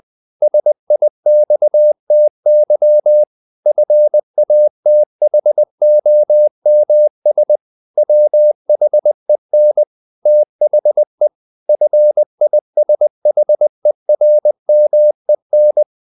Morse code, heard as a keyed tone, about 20 wpm, 600 Hz. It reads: SIXTY FATHOMS WHEN THE FISHERMEN
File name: the old man and the sea / the old man and the sea part 1 20wpm